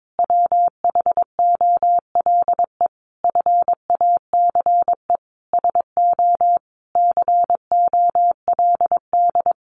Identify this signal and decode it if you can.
Morse code, heard as a keyed tone, about 22 wpm, 700 Hz. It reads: WHOLE FACE SO COLD
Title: Sentences from Top 1400 Words 22wpm